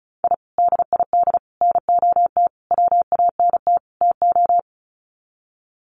Morse code, heard as a keyed tone, about 34 wpm, 700 Hz. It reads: I DID NOT WANT TO
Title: Sets of 5 Words Encore 35wpm